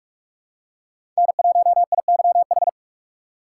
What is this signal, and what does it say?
Morse code, heard as a keyed tone, about 44 wpm, 700 Hz. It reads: N1IYH
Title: USA Call Signs - 5 Characters - 4x Character Spacing - 45wpm